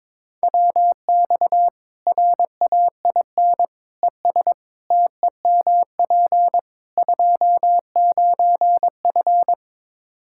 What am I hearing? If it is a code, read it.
Morse code, heard as a keyed tone, about 21 wpm, 700 Hz. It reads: WX RAIN ES TEMP 29F
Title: Practice QSO - Exchange 2 - Info - 22wpm